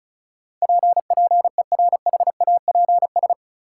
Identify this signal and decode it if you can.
Morse code, heard as a keyed tone, about 36 wpm, 700 Hz. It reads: P PERHAPS
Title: Sentences from Top 1100 Words - New Word Drill Alt - 35wpm